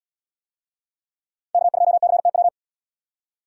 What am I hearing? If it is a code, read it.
Morse code, heard as a keyed tone, about 77 wpm, 700 Hz. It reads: K2CTW